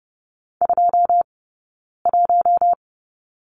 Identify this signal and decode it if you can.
Morse code, heard as a keyed tone, about 30 wpm, 700 Hz. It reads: 2 1
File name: Lesson 19d - Letter w - Two Characters 30wpm